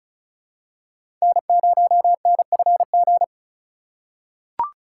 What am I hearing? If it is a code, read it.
Morse code, heard as a keyed tone, about 34 wpm, 700 Hz. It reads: N0NFG
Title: USA Call Signs - 5 Characters - 3x Character Spacing - 35wpm